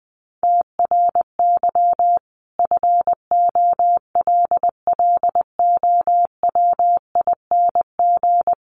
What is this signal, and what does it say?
Morse code, heard as a keyed tone, about 20 wpm, 700 Hz. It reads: TRY FOLLOWING
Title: Sentences from Top 1100 Words 20wpm